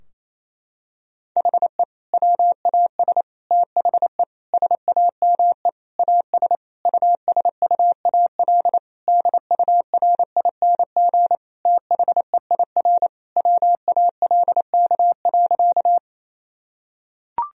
Morse code, heard as a keyed tone, about 28 wpm, 700 Hz. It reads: HE WAS THE SAME AS USUAL DURING THEIR WALK.
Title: Sentences from Top 1000 Words - Review Part 2 - 28wpm